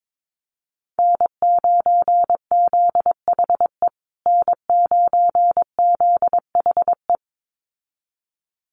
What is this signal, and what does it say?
Morse code, heard as a keyed tone, about 22 wpm, 700 Hz. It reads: N9ZHE N9ZHE
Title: USA Call Signs Repeated 2x 22wpm